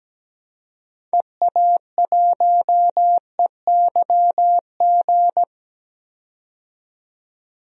Morse code, heard as a keyed tone, about 17 wpm, 700 Hz. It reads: EA1EYG
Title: DXCC-100 Call Signs - 6 Characters - 17wpm